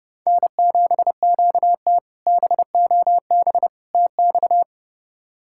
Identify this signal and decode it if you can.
Morse code, heard as a keyed tone, about 32 wpm, 700 Hz. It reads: N7QT BOB TX
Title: K1USN SST QSO Practice - 30wpm